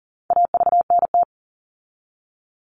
Morse code, heard as a keyed tone, about 40 wpm, 700 Hz. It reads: AVNT
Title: All Cut Numbers - 4-Digit 40wpm